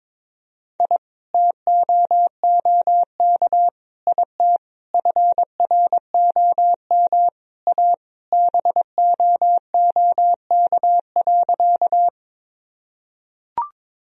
Morse code, heard as a keyed tone, about 21 wpm, 700 Hz. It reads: I TOOK IT FROM A BOOK.
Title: Sentences from Top 1000 Words - Review Part 1 - 22wpm